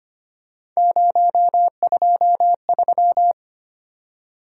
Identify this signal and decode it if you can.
Morse code, heard as a keyed tone, about 25 wpm, 700 Hz. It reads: 023
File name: Three Letter-Number Combinations - 25wpm